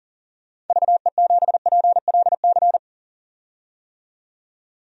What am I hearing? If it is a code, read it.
Morse code, heard as a keyed tone, about 40 wpm, 700 Hz. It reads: VE7PLC
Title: DXCC-100 Call Signs - 6 Characters - 40wpm